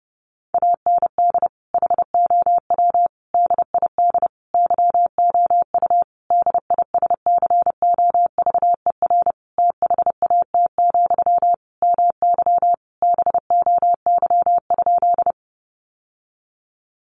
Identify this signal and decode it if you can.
Morse code, heard as a keyed tone, about 30 wpm, 700 Hz. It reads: AND HOW DID YOU DISCOVER THAT, MY BOY?
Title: Sentences from Top 1100 Words 30wpm